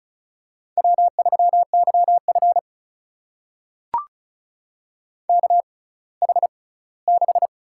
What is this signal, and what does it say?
Morse code, heard as a keyed tone, about 36 wpm, spaced as for 20 wpm, 700 Hz. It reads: W3YF KH6
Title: USA Call Signs - 4 Characters - 6x Character Spacing - 35wpm